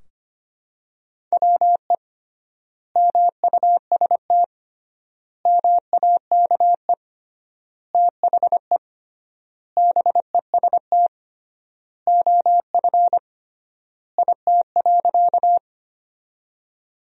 Morse code, heard as a keyed tone, about 25 wpm, 700 Hz. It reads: WE MUST MAKE THE BEST OF IT.
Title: 3x Word-Spacing - Sentences from Top 400 Words - 25wpm